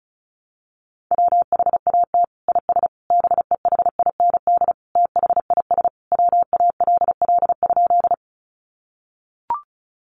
Morse code, heard as a keyed tone, about 34 wpm, 700 Hz. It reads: WHAT IS BEHIND THIS WALL?
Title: Sentences from Top 1100 Words 35wpm